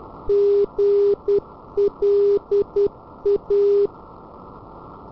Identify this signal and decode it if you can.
Morse code, heard as a keyed tone, about 10 wpm, 400 Hz. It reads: GLA
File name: radio tv historisch / AM Sender / GLA